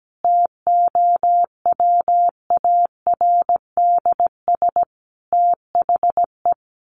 Morse code, heard as a keyed tone, about 17 wpm, 700 Hz. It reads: TOWARDS THE
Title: Additional Easy Phrases 17wpm